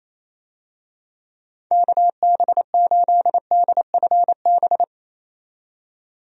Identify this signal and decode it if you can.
Morse code, heard as a keyed tone, about 28 wpm, 700 Hz. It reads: KB8DFB
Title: USA Call Signs - 6 Characters - 28wpm